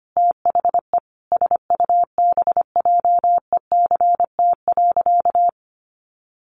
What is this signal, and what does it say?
Morse code, heard as a keyed tone, about 25 wpm, 700 Hz. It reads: THE SUBJECT.
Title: Sentences from Top 1400 Words 25wpm